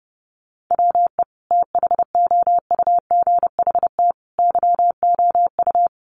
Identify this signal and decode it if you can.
Morse code, heard as a keyed tone, about 30 wpm, 700 Hz. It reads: WE THOUGHT YOU
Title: Sentences from Top 800 Words 30wpm